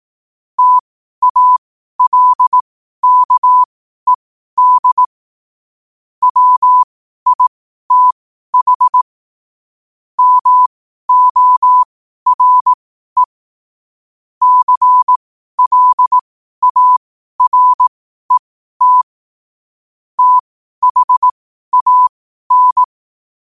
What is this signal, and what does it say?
Morse code, heard as a keyed tone, about 18 wpm, spaced as for 12 wpm, 1000 Hz. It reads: TALKED WITH MORE CLARET THAN
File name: qotdfiles / 12wpm